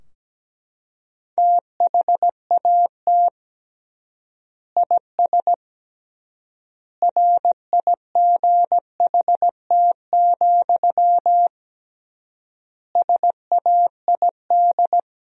Morse code, heard as a keyed tone, about 17 wpm, 700 Hz. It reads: THAT IS RIGHT, SAID
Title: 3x Word-Spacing - Sentences from Top 200 Words - 17wpm